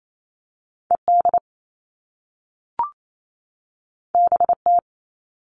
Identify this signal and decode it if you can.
Morse code, heard as a keyed tone, about 28 wpm, 700 Hz. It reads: ED BT
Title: All Cut Numbers - 2-Digit 28wpm